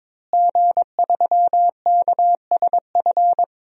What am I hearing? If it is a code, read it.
Morse code, heard as a keyed tone, about 22 wpm, 700 Hz. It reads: G3KSF
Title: DXCC-100 Call Signs - 5 Characters - Repeated 3x - 22wpm